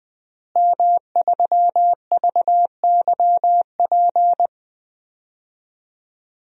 Morse code, heard as a keyed tone, about 20 wpm, 700 Hz. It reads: M3VYP
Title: DXCC-100 Call Signs - 5 Characters - Repeated 3x - 20wpm